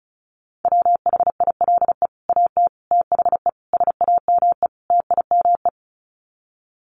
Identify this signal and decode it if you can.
Morse code, heard as a keyed tone, about 34 wpm, 700 Hz. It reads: WHILE AT THE SAME TIME
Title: Sets of 5 Words 35wpm